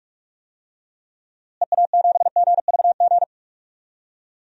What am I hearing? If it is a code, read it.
Morse code, heard as a keyed tone, about 43 wpm, 700 Hz. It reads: EA7GVG
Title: DXCC-100 Call Signs - 6 Characters - Repeated 3x - 45wpm